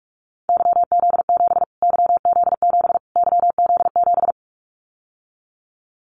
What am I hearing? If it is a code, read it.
Morse code, heard as a keyed tone, about 44 wpm, 700 Hz. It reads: YZ7 YZ7 YZ7
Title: Callsign Prefixes for Top 25 DXCC - Repeated 3x - 45wpm